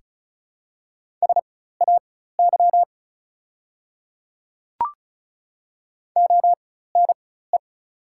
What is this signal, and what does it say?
Morse code, heard as a keyed tone, about 36 wpm, spaced as for 16 wpm, 690 Hz. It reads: SAY ONE
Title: 4x Character Spacing - Top 300 Words - 35wpm